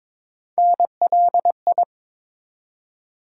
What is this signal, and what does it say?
Morse code, heard as a keyed tone, about 21 wpm, 700 Hz. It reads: NLI
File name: ARRL Sections v2 - 22wpm